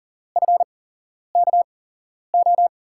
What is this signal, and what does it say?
Morse code, heard as a keyed tone, about 39 wpm, spaced as for 11 wpm, 700 Hz. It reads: FKO
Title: USA Call Signs - 6 Characters - 8x Character Spacing - 40wpm